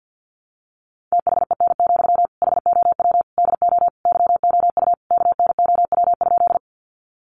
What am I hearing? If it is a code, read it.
Morse code, heard as a keyed tone, about 50 wpm, 700 Hz. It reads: THEN, HOW DO YOU KNOW?
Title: Sentences from Top 100 Words 50wpm